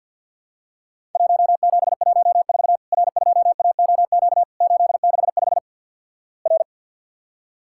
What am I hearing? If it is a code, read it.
Morse code, heard as a keyed tone, about 50 wpm, 700 Hz. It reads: 1714 R2AOQ 865 R
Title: Worked All Europe DX Contest - QTC Exchange - 50wpm